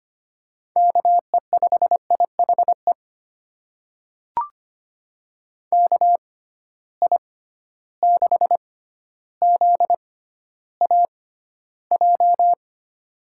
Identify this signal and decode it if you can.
Morse code, heard as a keyed tone, about 25 wpm, spaced as for 13 wpm, 700 Hz. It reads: KE5IHE KI6ZAJ
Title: USA Call Signs - 6 Characters - 6x Character Spacing - 25wpm